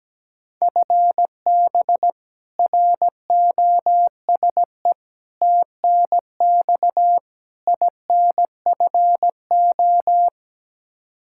Morse code, heard as a keyed tone, about 17 wpm, 700 Hz. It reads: FB ROSE TNX INFO
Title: Practice QSO - Exchange 3 - More Info - 17wpm